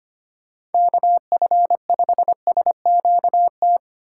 Morse code, heard as a keyed tone, about 25 wpm, 700 Hz. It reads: KF5SQT
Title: USA Call Signs - 3x Character Spacing - 25wpm